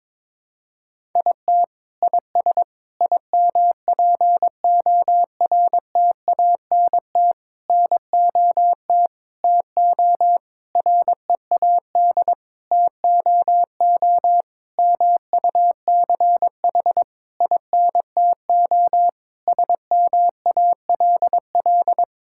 Morse code, heard as a keyed tone, about 22 wpm, 700 Hz. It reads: IT IS IMPORTANT NOT TO READ TOO MUCH INTO SMALL
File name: Sentences with 9-Word Phrases - 22wpm